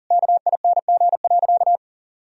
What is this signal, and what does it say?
Morse code, heard as a keyed tone, about 40 wpm, 700 Hz. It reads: KING.
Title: Sentences from Top 1000 Words - Review Part 3 - 40wpm